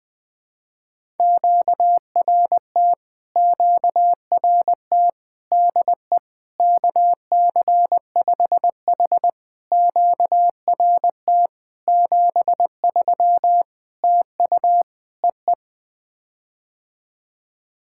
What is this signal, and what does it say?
Morse code, heard as a keyed tone, about 20 wpm, 700 Hz. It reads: QRT QRT DE KC5H QRT 73 TU EE
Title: POTA QSO Practice - 20wpm